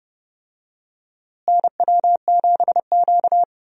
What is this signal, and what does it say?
Morse code, heard as a keyed tone, about 30 wpm, 700 Hz. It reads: NW7Q
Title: USA Call Signs - 4 Characters - Repeated 3x - 30wpm